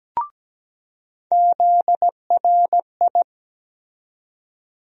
Morse code, heard as a keyed tone, about 17 wpm, 700 Hz. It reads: ZRI
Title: USA Call Sign Postfixes - 3 Letters 17wpm